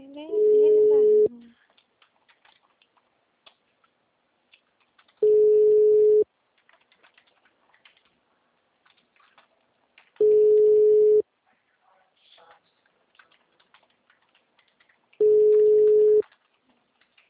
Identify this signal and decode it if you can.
Ringback tone (Europe, 425 Hz), with speech.